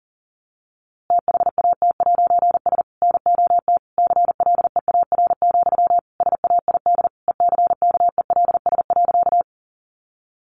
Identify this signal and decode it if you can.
Morse code, heard as a keyed tone, about 40 wpm, 700 Hz. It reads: THAT'S NOT CLEAR, SAID ECKELS.
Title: A Sound Of Thunder 40wpm